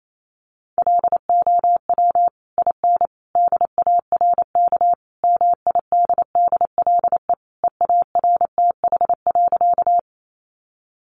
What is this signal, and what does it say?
Morse code, heard as a keyed tone, about 28 wpm, 700 Hz. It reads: LOW IN DARK MIDDLE EARTH.
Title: Sentences from Top 900 Words 28wpm